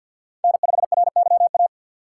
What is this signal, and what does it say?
Morse code, heard as a keyed tone, about 49 wpm, 700 Hz. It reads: N5RYA
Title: USA Call Signs - 5 Characters - 6x Character Spacing - 50wpm